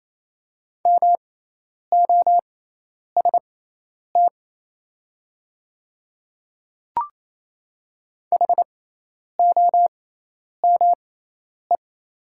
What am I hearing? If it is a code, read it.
Morse code, heard as a keyed tone, about 28 wpm, spaced as for 10 wpm, 700 Hz. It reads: MOST HOME
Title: 6x Character Spacing - Top 200 Words - 28wpm